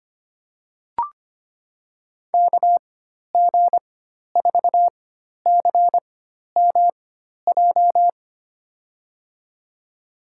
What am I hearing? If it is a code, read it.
Morse code, heard as a keyed tone, about 25 wpm, spaced as for 12 wpm, 700 Hz. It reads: KG4CMJ
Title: USA Call Signs - 6 Characters - 4x Character Spacing - 25wpm